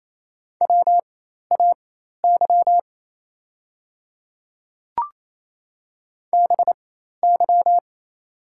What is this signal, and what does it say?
Morse code, heard as a keyed tone, about 27 wpm, spaced as for 13 wpm, 690 Hz. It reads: WAY BY
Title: 4x Character Spacing - Top 100 Words - 28wpm